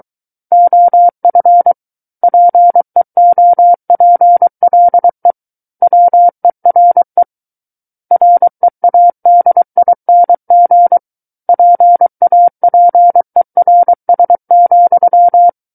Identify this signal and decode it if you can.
Morse code, heard as a keyed tone, about 23 wpm, 700 Hz. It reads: OF PEOPLE WERE READING PAPERS,
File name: Book One / 14-Chapter 14 - 23 WPM